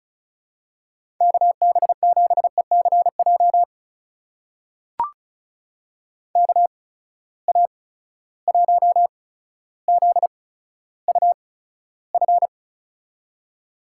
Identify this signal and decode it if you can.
Morse code, heard as a keyed tone, about 36 wpm, spaced as for 15 wpm, 700 Hz. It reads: KB7ECJ KA1ZUF
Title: USA Call Signs - 6 Characters - 8x Character Spacing - 35wpm